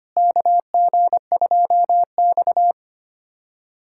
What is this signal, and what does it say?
Morse code, heard as a keyed tone, about 25 wpm, 700 Hz. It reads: KG2X
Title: USA Callsign DE Callsign - 4 Characters - 25wpm